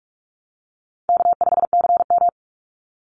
Morse code, heard as a keyed tone, about 44 wpm, 700 Hz. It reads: K5CM
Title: Easy Call Signs Repeated 2x - 45wpm